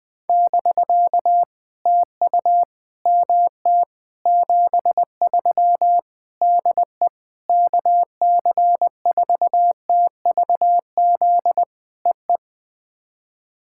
Morse code, heard as a keyed tone, about 20 wpm, 700 Hz. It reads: <BK> TU MT 73 DE KC4TVZ EE
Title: POTA QSO Practice - 20wpm